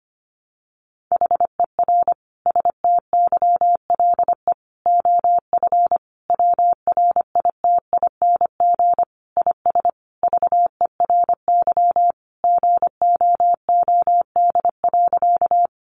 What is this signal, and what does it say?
Morse code, heard as a keyed tone, about 25 wpm, 700 Hz. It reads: HER STYLE OF WRITING IS VERY GOOD.
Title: Sentences from Top 800 Words 25wpm